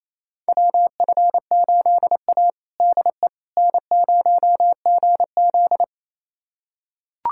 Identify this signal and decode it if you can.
Morse code, heard as a keyed tone, about 27 wpm, 700 Hz. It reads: WF8A DE N0GZ
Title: USA Callsign DE Callsign - 4 Characters - 28wpm